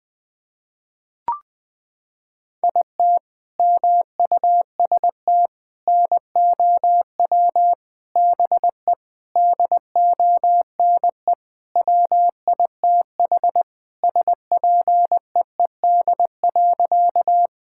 Morse code, heard as a keyed tone, about 21 wpm, 700 Hz. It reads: IT MUST NOW BE DONE WITH SPEED.
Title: Sentences from Top 1000 Words - Review Part 5 - 20wpm